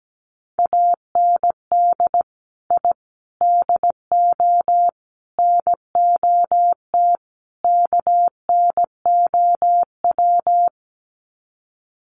Morse code, heard as a keyed tone, about 17 wpm, 700 Hz. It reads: AND I DO NOT KNOW
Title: Sets of 5 Words 17wpm